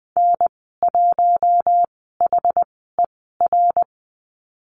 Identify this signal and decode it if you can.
Morse code, heard as a keyed tone, about 20 wpm, spaced as for 14 wpm, 700 Hz. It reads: N1HER
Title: 069 Lesson 12g - Letter d - Call Signs 20wpm